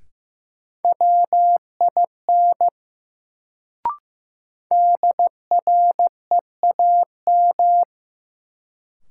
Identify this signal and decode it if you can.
Morse code, heard as a keyed tone, about 15 wpm, 700 Hz. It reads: WIN DREAM
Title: Sentences from Top 1500 Words - New Word Drill Alt - 15wpm